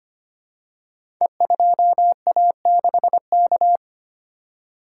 Morse code, heard as a keyed tone, about 25 wpm, 700 Hz. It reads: E2A6K
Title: Code Groups - 5 Letter:Numbers - 25wpm